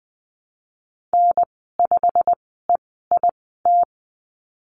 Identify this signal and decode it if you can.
Morse code, heard as a keyed tone, about 20 wpm, spaced as for 15 wpm, 700 Hz. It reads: N5EIT
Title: 088 Lesson 15g - Number 5 - Call Signs 20wpm